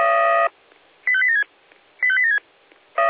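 Five-tone selective call (ZVEI), heard as quick runs of five tones, tones of about 70 ms. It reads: 85876 85876
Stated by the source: ZVEI five-tone call received on SDR.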